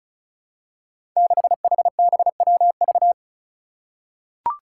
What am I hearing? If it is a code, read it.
Morse code, heard as a keyed tone, about 36 wpm, 700 Hz. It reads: 6HBWV
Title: Five Challenging Letter-Number Combinations 35wpm